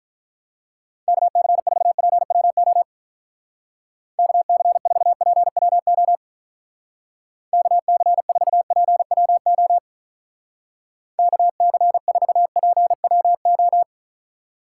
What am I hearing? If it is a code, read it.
Morse code, heard as a keyed tone, about 42 wpm, 700 Hz. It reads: KC4PWO KC4PWO KC4PWO KC4PWO
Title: Speed Racing - USA 2x3 Call Signs 35wpm